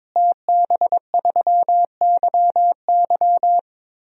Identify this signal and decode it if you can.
Morse code, heard as a keyed tone, about 22 wpm, 700 Hz. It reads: TB3YY
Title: DXCC-100 Call Signs - 5 Characters - Repeated 2x - 22wpm